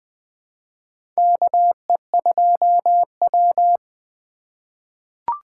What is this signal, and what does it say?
Morse code, heard as a keyed tone, about 20 wpm, 700 Hz. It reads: KE2W
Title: USA Call Signs - 3x Character Spacing - 20wpm